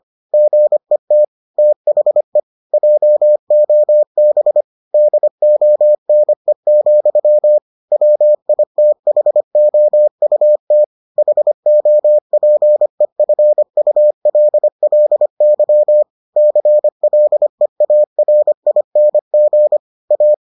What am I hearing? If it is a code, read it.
Morse code, heard as a keyed tone, about 25 wpm, 600 Hz. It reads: GET THE JOB DONE, WITHOUT HOPEFULLY CLEARING A